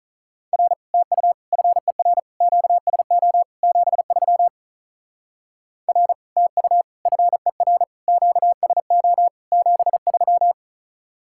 Morse code, heard as a keyed tone, about 44 wpm, 700 Hz. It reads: R TU FER QSO 73 R TU FER QSO 73